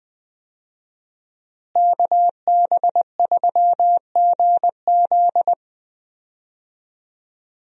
Morse code, heard as a keyed tone, about 21 wpm, 700 Hz. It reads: KB3GZ
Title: USA Call Signs - 5 Characters - Repeated 2x - 20wpm